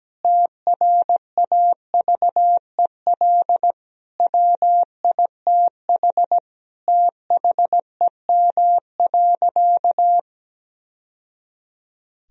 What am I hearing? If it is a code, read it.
Morse code, heard as a keyed tone, about 17 wpm, 700 Hz. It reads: TRAVEL WITH THEM.
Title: Sentences from Top 1000 Words - Review Part 2 - 17wpm